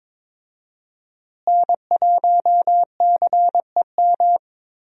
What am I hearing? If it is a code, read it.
Morse code, heard as a keyed tone, about 22 wpm, 700 Hz. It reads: N1CEM
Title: USA Call Signs - 5 Characters - Repeated 2x - 22wpm